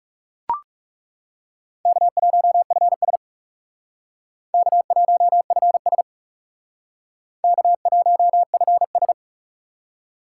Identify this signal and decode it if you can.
Morse code, heard as a keyed tone, about 37 wpm, 700 Hz. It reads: K1FS K1FS K1FS
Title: Speed Racing - USA 1x2 Call Signs 30wpm